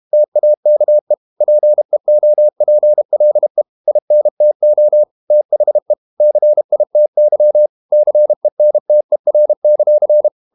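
Morse code, heard as a keyed tone, about 32 wpm, 600 Hz. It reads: TAKE PEOPLE INTO THE CITY CENTER;
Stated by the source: Morse code practice at 32 wpm from CW med Gnister, reading Ken Follett's novel Never in English.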